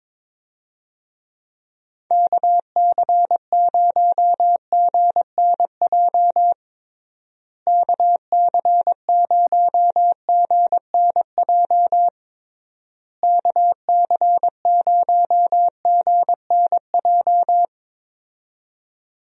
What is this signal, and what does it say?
Morse code, heard as a keyed tone, about 21 wpm, 700 Hz. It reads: KC0GNJ KC0GNJ KC0GNJ
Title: USA Call Signs - 6 Characters - Repeated 3x - 22wpm